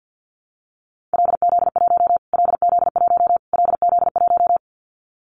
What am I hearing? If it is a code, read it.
Morse code, heard as a keyed tone, about 50 wpm, 700 Hz. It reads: LZ1 LZ1 LZ1
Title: Callsign Prefixes for Top 25 DXCC - Repeated 3x - 50wpm